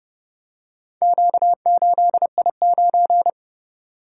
Morse code, heard as a keyed tone, about 28 wpm, 700 Hz. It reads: Q8I9